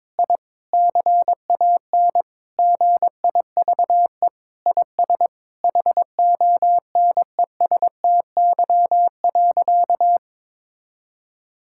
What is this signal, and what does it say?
Morse code, heard as a keyed tone, about 22 wpm, 700 Hz. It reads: I CAN GIVE IS HONESTY.
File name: practice / 22wpm / Sentences with 8-Word Phrases - 22wpm